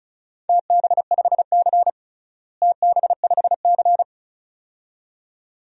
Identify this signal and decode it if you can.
Morse code, heard as a keyed tone, about 36 wpm, 700 Hz. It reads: TB5C TB5C
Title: DXCC-100 Call Signs - 4 Characters - Repeated 2x - 35wpm